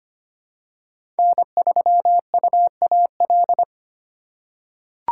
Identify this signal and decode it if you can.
Morse code, heard as a keyed tone, about 25 wpm, 700 Hz. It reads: N3UAL
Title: USA Call Signs - 5 Characters - 4x Character Spacing - 25wpm